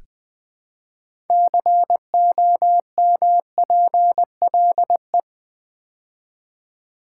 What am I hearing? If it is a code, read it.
Morse code, heard as a keyed tone, about 20 wpm, 700 Hz. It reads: COMPLE
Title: Build A Word - 3 Letters At A Time - 8-Letter Words - 20wpm